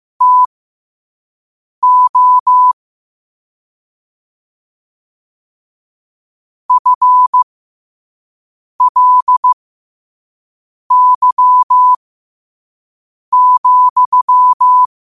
Morse code, heard as a keyed tone, about 15 wpm, spaced as for 5 wpm, 1000 Hz. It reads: TO FLY,